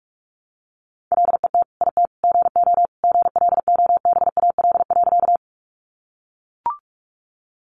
Morse code, heard as a keyed tone, about 44 wpm, 700 Hz. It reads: LET IT GO GLOBAL.
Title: Sentences from Top 1200 Words 45wpm